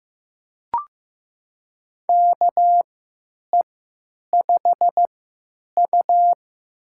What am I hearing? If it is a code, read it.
Morse code, heard as a keyed tone, about 15 wpm, spaced as for 9 wpm, 700 Hz. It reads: KE5U
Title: USA Call Signs - 4 Characters - 3x Character Spacing - 15wpm